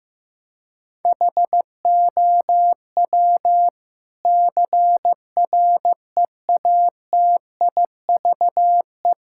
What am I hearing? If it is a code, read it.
Morse code, heard as a keyed tone, about 15 wpm, 700 Hz. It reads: HOW CREATIVE
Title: Sentences from Top 1000 Words - Review Part 5 - 15wpm